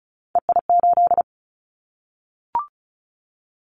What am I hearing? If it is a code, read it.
Morse code, heard as a keyed tone, about 34 wpm, 700 Hz. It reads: EI8
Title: Callsign Prefixes for Top 50 DXCC - 35wpm